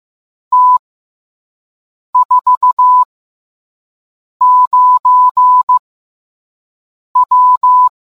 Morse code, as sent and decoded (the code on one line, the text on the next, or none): - / ....- / ----. / .--
T 4 9 W